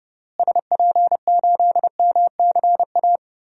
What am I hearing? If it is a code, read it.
Morse code, heard as a keyed tone, about 30 wpm, 700 Hz. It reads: SP8MCA
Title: DXCC-100 Call Signs - 6 Characters - Repeated 3x - 30wpm